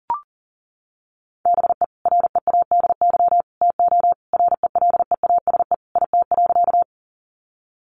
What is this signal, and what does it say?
Morse code, heard as a keyed tone, about 37 wpm, 700 Hz. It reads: BE READY TO RELEASE IT.
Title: Sentences from Top 1100 Words 40wpm